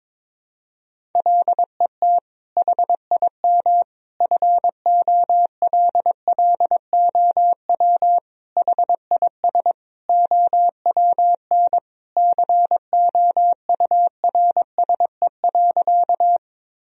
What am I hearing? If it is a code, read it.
Morse code, heard as a keyed tone, about 22 wpm, 700 Hz. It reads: LET HIM FOLLOW HIS OWN COURSE.
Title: Sentences from Top 1000 Words - Review Part 1 - 22wpm